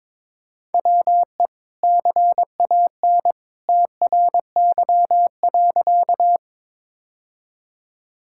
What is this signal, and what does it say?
Morse code, heard as a keyed tone, about 21 wpm, 700 Hz. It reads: WE CAN TRY.
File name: practice / 22wpm / Sentences with 2-Word Phrases - 22wpm